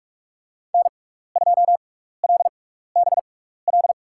Morse code, heard as a keyed tone, about 47 wpm, spaced as for 16 wpm, 690 Hz. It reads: N2LBL
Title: USA Call Signs - 5 Characters - 6x Character Spacing - 45wpm